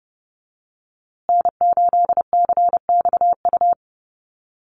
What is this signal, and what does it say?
Morse code, heard as a keyed tone, about 30 wpm, 700 Hz. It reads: N8CXU
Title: USA Call Signs Repeated 2x 30wpm